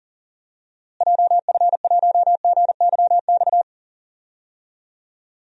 Morse code, heard as a keyed tone, about 40 wpm, 700 Hz. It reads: JF1GYX
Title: DXCC-100 Call Signs - 6 Characters - Repeated 3x - 40wpm